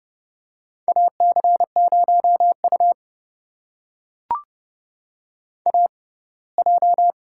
Morse code, heard as a keyed tone, about 30 wpm, spaced as for 20 wpm, 700 Hz. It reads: AC0U AJ
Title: USA Call Signs - 4 Characters - 6x Character Spacing - 30wpm